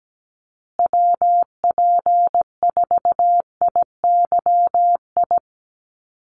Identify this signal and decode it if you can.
Morse code, heard as a keyed tone, about 17 wpm, 700 Hz. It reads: WP4IYI
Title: USA Call Signs Repeated 3x 17wpm